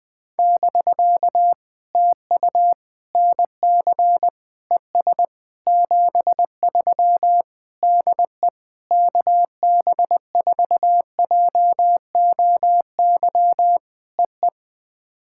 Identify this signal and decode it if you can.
Morse code, heard as a keyed tone, about 20 wpm, 700 Hz. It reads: <BK> TU NC ES 73 DE KB4JOY EE
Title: POTA QSO Practice - 20wpm